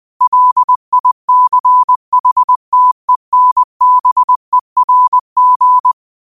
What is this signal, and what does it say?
Morse code, heard as a keyed tone, about 20 wpm, 1000 Hz. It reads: LICHTENBERG
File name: 2025-06-25-20wpm